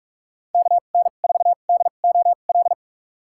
Morse code, heard as a keyed tone, about 41 wpm, spaced as for 31 wpm, 700 Hz. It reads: KN4DOL
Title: USA Call Signs - 6 Characters - 2x Character Spacing - 45wpm